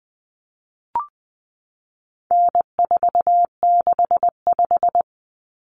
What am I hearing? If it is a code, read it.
Morse code, heard as a keyed tone, about 20 wpm, 700 Hz. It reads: N465
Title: Common Cut Numbers - 4-Digit 20wpm